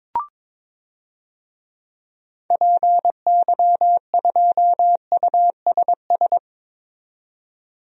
Morse code, heard as a keyed tone, about 21 wpm, 700 Hz. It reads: PY2USS
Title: DXCC-100 Call Signs - 6 Characters - 22wpm